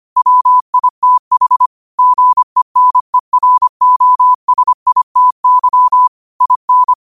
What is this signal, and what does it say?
Morse code, heard as a keyed tone, about 25 wpm, 1000 Hz. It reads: WITH GENEROSITY IN
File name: qotdfiles / 25wpm